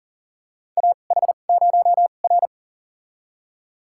Morse code, heard as a keyed tone, about 38 wpm, spaced as for 28 wpm, 700 Hz. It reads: AH0R
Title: USA Call Signs - 4 Characters - 2x Character Spacing - 40wpm